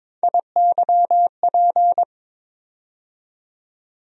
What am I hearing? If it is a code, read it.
Morse code, heard as a keyed tone, about 23 wpm, 700 Hz. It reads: IYP
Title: USA Call Sign Postfixes - 3 Letters 22wpm